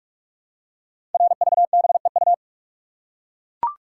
Morse code, heard as a keyed tone, about 44 wpm, 700 Hz. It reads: RVBEU
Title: Code Groups - 5 Letters - 45wpm